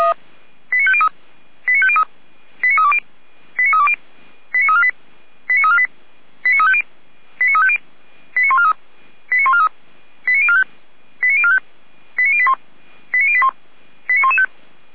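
Five-tone selective call (ZVEI), heard as quick runs of five tones, tones of about 70 ms. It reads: 89503 89503 89320 89320 89348 89348 89350 89350 89143 89143 89045 89045 89081 89081 89115
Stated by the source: ZVEI five-tone selective call received on an SDR.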